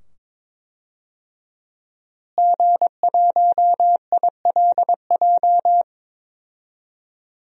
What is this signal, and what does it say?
Morse code, heard as a keyed tone, about 22 wpm, 700 Hz. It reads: G1ILJ
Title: DXCC-100 Call Signs - 5 Characters - Repeated 2x - 22wpm